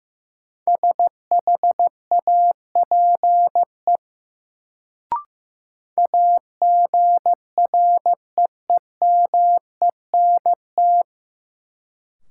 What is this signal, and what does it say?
Morse code, heard as a keyed tone, about 15 wpm, 700 Hz. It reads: SHAPE AGREEMENT
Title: Sets of 5 Words Encore - New Word Drill - 15wpm